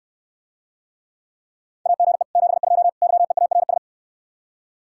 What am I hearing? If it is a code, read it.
Morse code, heard as a keyed tone, about 69 wpm, 700 Hz. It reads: AGE 72 YEARS